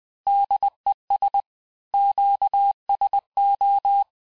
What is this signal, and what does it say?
Morse code, heard as a keyed tone, about 20 wpm, 800 Hz. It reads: DES QSO
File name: cours-du-lundi files / cours-1584-Fin-de-QSO-vitesse-19mpm